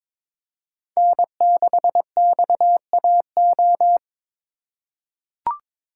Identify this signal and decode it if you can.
Morse code, heard as a keyed tone, about 21 wpm, 700 Hz. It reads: N6XAO
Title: USA Call Signs - 5 Characters - 6x Character Spacing - 22wpm